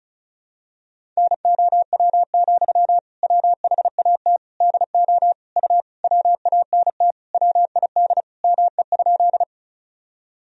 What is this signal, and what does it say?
Morse code, heard as a keyed tone, about 34 wpm, 700 Hz. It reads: NOW, WHAT DO U WANT WID ME?
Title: Sentences from Top 200 Words and Abbreviations 35wpm